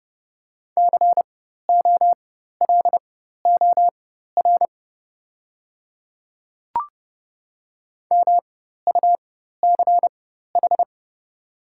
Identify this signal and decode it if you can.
Morse code, heard as a keyed tone, about 30 wpm, spaced as for 14 wpm, 700 Hz. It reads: COLOR MUCH
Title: 4x Character Spacing - Top 300 Words - 30wpm